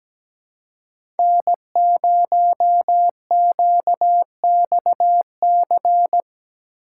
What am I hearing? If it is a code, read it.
Morse code, heard as a keyed tone, about 17 wpm, 700 Hz. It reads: N0QXC
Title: USA Call Signs - 5 Characters - 4x Character Spacing - 17wpm